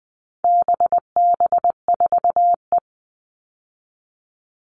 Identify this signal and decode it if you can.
Morse code, heard as a keyed tone, about 20 wpm, 700 Hz. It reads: BB4E
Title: All Cut Numbers - 4-Digit 20wpm